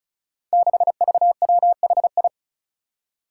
Morse code, heard as a keyed tone, about 36 wpm, 700 Hz. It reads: 6VWHI
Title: Five Challenging Letter-Number Combinations 35wpm